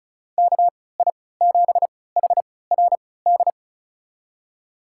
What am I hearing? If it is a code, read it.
Morse code, heard as a keyed tone, about 36 wpm, spaced as for 20 wpm, 700 Hz. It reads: KI7HRD
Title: USA Call Signs - 6 Characters - 3x Character Spacing - 35wpm